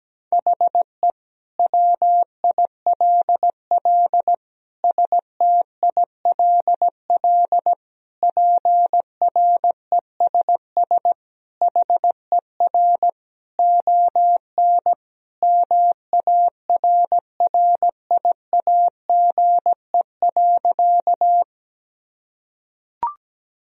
Morse code, heard as a keyed tone, about 17 wpm, 700 Hz. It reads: HE WILL STILL PRESS HER ON MARRIAGE.
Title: Sentences from Top 1000 Words - Review Part 1 - 17wpm